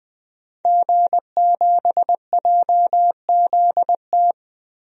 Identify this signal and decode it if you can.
Morse code, heard as a keyed tone, about 20 wpm, 700 Hz. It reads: G7JZT
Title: DXCC-100 Call Signs - 5 Characters - Repeated 2x - 20wpm